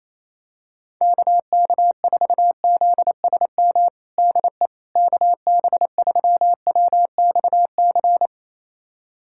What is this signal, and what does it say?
Morse code, heard as a keyed tone, about 28 wpm, 700 Hz. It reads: KK4ZSM DE KB3WXC
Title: USA Callsign DE Callsign - 6 Characters - 28wpm